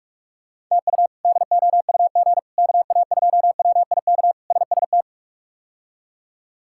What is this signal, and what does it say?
Morse code, heard as a keyed tone, about 46 wpm, 700 Hz. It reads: TU DOUG KA2WIK SST
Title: K1USN SST QSO Practice - 45wpm